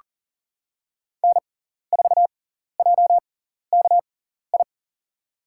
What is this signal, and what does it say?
Morse code, heard as a keyed tone, about 39 wpm, spaced as for 14 wpm, 700 Hz. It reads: N4JKI
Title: USA Call Signs - 6x Character Spacing - 40wpm